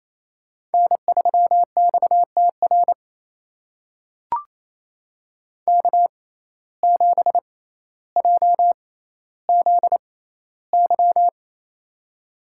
Morse code, heard as a keyed tone, about 28 wpm, spaced as for 14 wpm, 700 Hz. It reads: N3XTR K7JZY
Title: USA Call Signs - 5 Characters - 6x Character Spacing - 28wpm